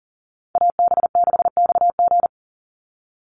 Morse code, heard as a keyed tone, about 40 wpm, 700 Hz. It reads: AB6XG
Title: Call Signs - 40wpm